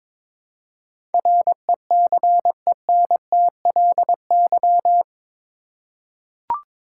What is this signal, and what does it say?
Morse code, heard as a keyed tone, about 22 wpm, 700 Hz. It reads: RECENTLY
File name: practice / 22wpm / Eight Letter Words - 22wpm